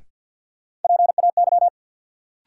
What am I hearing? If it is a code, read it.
Morse code, heard as a keyed tone, about 48 wpm, 700 Hz. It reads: PAY
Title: Sentences from Top 700 Words - New Word Drill - 50wpm